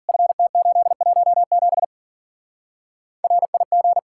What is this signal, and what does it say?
Morse code, heard as a keyed tone, about 49 wpm, 700 Hz. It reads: FT817 RIG
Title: Speed-Racing Common QSO Phrases Encore - 40wpm